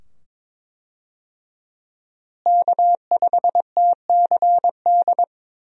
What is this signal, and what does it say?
Morse code, heard as a keyed tone, about 22 wpm, 700 Hz. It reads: K5TCD
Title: USA Call Signs - 5 Characters - Repeated 2x - 22wpm